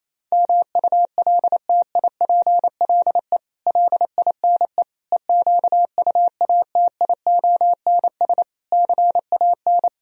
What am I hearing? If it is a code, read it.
Morse code, heard as a keyed tone, about 26 wpm, 700 Hz. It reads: MULTIPLE LINE EQUATIONS CAN
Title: Speed-Racing - Sets of 5 Words 28wpm